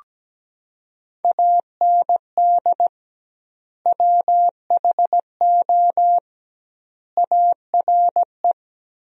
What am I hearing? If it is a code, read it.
Morse code, heard as a keyed tone, about 17 wpm, 700 Hz. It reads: AND WHO ARE
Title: 2x Word-Spacing - Sentences from Top 100 Words - 17wpm